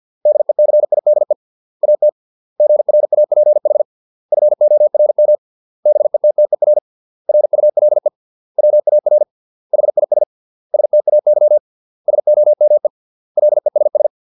Morse code, heard as a keyed tone, about 50 wpm, 600 Hz. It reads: DECIDE AT GRAPH FORM BETTER RULE WAR HIS STAY SOME LESS